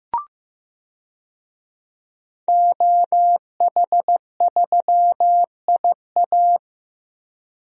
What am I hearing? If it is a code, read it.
Morse code, heard as a keyed tone, about 16 wpm, 700 Hz. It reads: OH3IA
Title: DXCC-100 Call Signs - 5 Characters - Repeated 3x - 15wpm